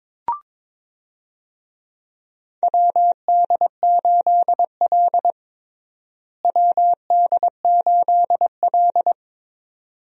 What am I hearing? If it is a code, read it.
Morse code, heard as a keyed tone, about 22 wpm, 700 Hz. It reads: WD8L WD8L
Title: USA Call Signs - 4 Characters - Repeated 2x - 22wpm